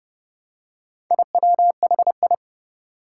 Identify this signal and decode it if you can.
Morse code, heard as a keyed tone, about 30 wpm, 700 Hz. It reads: IWHI
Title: Four Challenging Letter-Number Combinations 30wpm